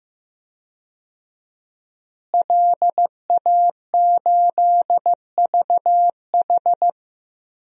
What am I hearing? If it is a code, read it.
Morse code, heard as a keyed tone, about 15 wpm, 700 Hz. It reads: LA8VH
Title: DXCC-100 Call Signs - 5 Characters - Repeated 2x - 15wpm